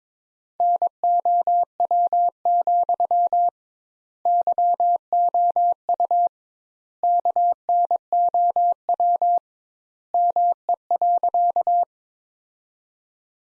Morse code, heard as a keyed tone, about 22 wpm, 700 Hz. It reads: NOW, YOU KNOW ME.
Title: Sentences Top 200 Words - 22wpm